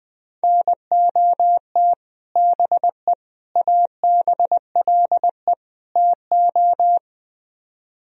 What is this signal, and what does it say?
Morse code, heard as a keyed tone, about 22 wpm, 700 Hz. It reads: NOT BE ABLE TO
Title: Speed-Racing - Sets of 4 Words 20wpm